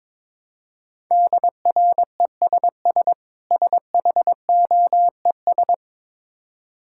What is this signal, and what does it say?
Morse code, heard as a keyed tone, about 22 wpm, 690 Hz. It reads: DRESS SHOES
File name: Difficult Words with Hints - 22wpm